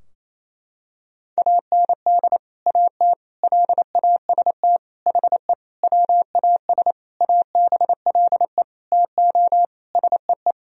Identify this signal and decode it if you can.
Morse code, heard as a keyed tone, about 28 wpm, 700 Hz. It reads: AND AT LAST HE WAS ABLE TO SEE
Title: Sentences from Top 1000 Words - Review Part 2 - 28wpm